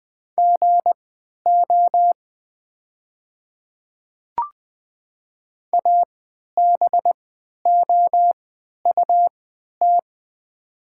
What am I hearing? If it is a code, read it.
Morse code, heard as a keyed tone, about 20 wpm, spaced as for 11 wpm, 700 Hz. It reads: GO ABOUT
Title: 3x Character Spacing - Top 100 Words - 20wpm